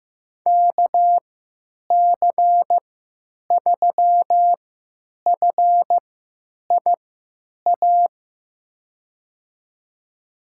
Morse code, heard as a keyed tone, about 15 wpm, spaced as for 9 wpm, 700 Hz. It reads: KC3FIA
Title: USA Call Signs - 6 Characters - 3x Character Spacing - 15wpm